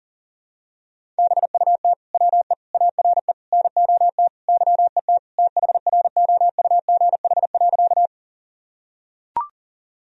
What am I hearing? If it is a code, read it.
Morse code, heard as a keyed tone, about 40 wpm, 700 Hz. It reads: BUT WE ARE NOT YET THROUGH.
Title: Sentences from Top 1000 Words - Review Part 1 - 40wpm